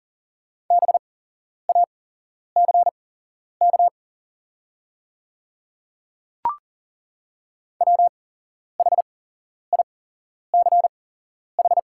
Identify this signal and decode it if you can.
Morse code, heard as a keyed tone, about 38 wpm, spaced as for 11 wpm, 700 Hz. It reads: BACK WHICH
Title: 8x Character Spacing - Top 200 Words - 40wpm